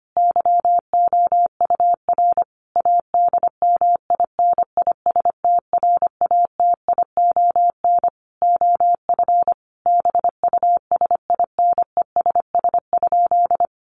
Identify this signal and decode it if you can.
Morse code, heard as a keyed tone, about 25 wpm, 700 Hz. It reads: YOUR ADMINISTRATION OF BUSINESS?
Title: Sentences from Top 1300 Words 25wpm